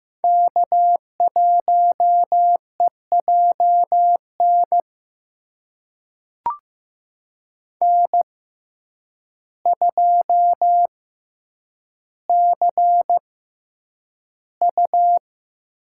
Morse code, heard as a keyed tone, about 15 wpm, spaced as for 8 wpm, 700 Hz. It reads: K1EJN N2CU
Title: USA Call Signs - 5 Characters - 6x Character Spacing - 15wpm